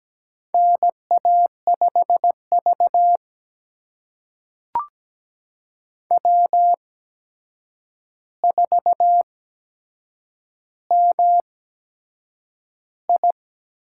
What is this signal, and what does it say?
Morse code, heard as a keyed tone, about 17 wpm, spaced as for 7 wpm, 700 Hz. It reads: NA5V W4MI
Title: USA Call Signs - 4 Characters - 8x Character Spacing - 17wpm